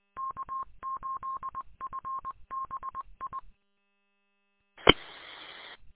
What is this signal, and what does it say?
Morse code, heard as a keyed tone, about 22 wpm, 1070 Hz. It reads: K8FBI E
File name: k8fbi-43732-combined